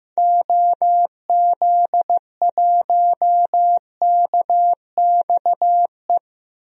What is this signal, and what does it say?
Morse code, heard as a keyed tone, about 16 wpm, 700 Hz. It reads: OZ1KXE
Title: DXCC-100 Call Signs - 6 Characters - 15wpm